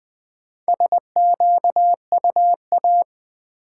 Morse code, heard as a keyed tone, about 20 wpm, 700 Hz. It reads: SQUA
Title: Build A Word - 1 Letter At A Time - 5-Letter Words - 20wpm